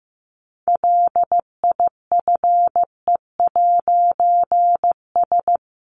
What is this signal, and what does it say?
Morse code, heard as a keyed tone, about 15 wpm, 700 Hz. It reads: LIFE'S
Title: A Sound Of Thunder 15wpm